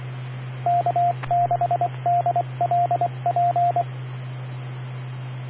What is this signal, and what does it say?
Morse code, heard as a keyed tone, about 24 wpm, 700 Hz, with radio noise. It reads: K6DLP
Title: Claremont, CA Police, Amateur Radio